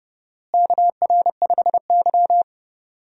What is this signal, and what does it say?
Morse code, heard as a keyed tone, about 31 wpm, 700 Hz. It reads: KR5Y
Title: USA Call Signs - 4 Characters - Repeated 2x - 30wpm